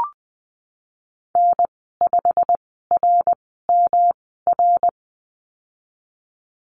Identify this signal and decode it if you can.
Morse code, heard as a keyed tone, about 20 wpm, spaced as for 15 wpm, 700 Hz. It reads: N5RMR
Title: 173 Lesson 28g - Number 9 - Call Signs 20wpm